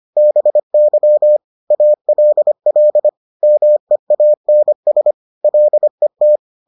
Morse code, heard as a keyed tone, about 25 wpm, 600 Hz. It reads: BY ALL MEANS LET